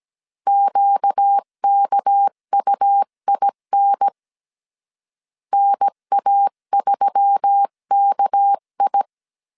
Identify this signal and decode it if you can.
Morse code, heard as a keyed tone, about 17 wpm, 800 Hz. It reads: QKUIN NA3KI
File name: CW / DET / 1 6 1500C S C 16wpm 2733